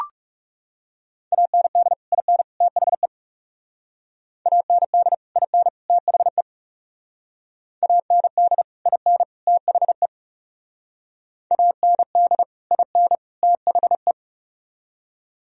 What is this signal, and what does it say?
Morse code, heard as a keyed tone, about 37 wpm, 700 Hz. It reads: AND IN THE AND IN THE AND IN THE AND IN THE
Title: Speed-Racing - Sets of 3 Words 30wpm